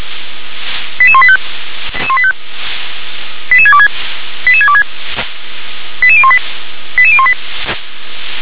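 Five-tone selective call (ZVEI), heard as quick runs of five tones, tones of about 70 ms. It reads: 88185 88626 88626 88018 88018
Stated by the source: ZVEI five-tone selective call 88625 received on SDR.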